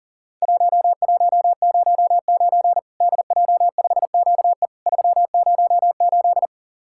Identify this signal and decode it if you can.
Morse code, heard as a keyed tone, about 40 wpm, 700 Hz. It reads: 1109 DJ5QE 308
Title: Worked All Europe DX Contest - QTC Exchange - 40wpm